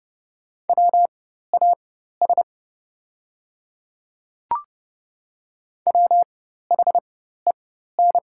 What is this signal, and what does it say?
Morse code, heard as a keyed tone, about 30 wpm, spaced as for 14 wpm, 700 Hz. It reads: WAS WHEN
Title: 4x Character Spacing - Top 100 Words - 30wpm